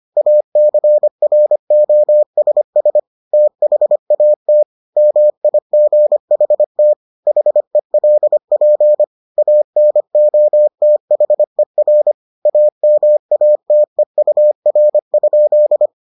Morse code, heard as a keyed tone, about 25 wpm, 600 Hz. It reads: ACROSS THAT MIGHT HELP ANOTHER AMATEUR?